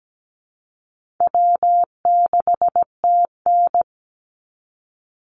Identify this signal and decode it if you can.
Morse code, heard as a keyed tone, about 17 wpm, 700 Hz. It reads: W6TN
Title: Easy Call Signs - 17wpm